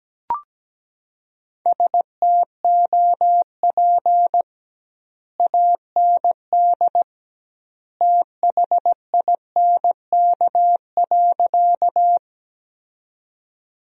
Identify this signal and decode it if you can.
Morse code, heard as a keyed tone, about 17 wpm, 700 Hz. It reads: STOP AND THINK.
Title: 2x Word-Spacing - Sentences from Top 300 Words - 17wpm